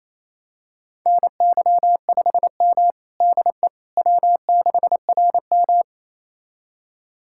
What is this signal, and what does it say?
Morse code, heard as a keyed tone, about 28 wpm, 700 Hz. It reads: NY5M DE W6RM
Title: USA Callsign DE Callsign - 4 Characters - 28wpm